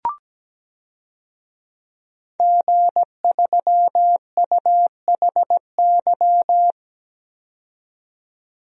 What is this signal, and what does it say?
Morse code, heard as a keyed tone, about 17 wpm, 700 Hz. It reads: G3UHY
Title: DXCC-100 Call Signs - 5 Characters - 17wpm